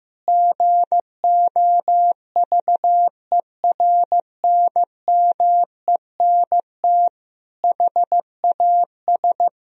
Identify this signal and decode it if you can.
Morse code, heard as a keyed tone, about 15 wpm, 700 Hz. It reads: GOVERNMENT HAS
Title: Sentences from Top 1000 Words - Review Part 3 - 15wpm